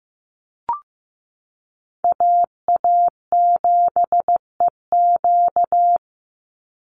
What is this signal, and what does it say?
Morse code, heard as a keyed tone, about 15 wpm, 700 Hz. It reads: AA7EQ
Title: NA SOTA Chaser Callsigns 15wpm